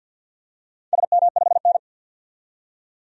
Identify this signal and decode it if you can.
Morse code, heard as a keyed tone, about 51 wpm, 700 Hz. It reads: SM5N
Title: DXCC-100 Call Signs - 4 Characters - 50wpm